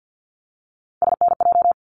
Morse code, heard as a keyed tone, about 50 wpm, 700 Hz. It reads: SNJ
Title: ARRL Sections 50wpm